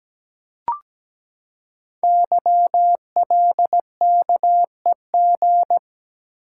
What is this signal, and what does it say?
Morse code, heard as a keyed tone, about 17 wpm, 700 Hz. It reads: YLKEG
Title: Code Groups - 5 Letters - 17wpm